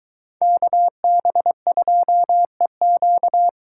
Morse code, heard as a keyed tone, about 22 wpm, 700 Hz. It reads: KB2EQ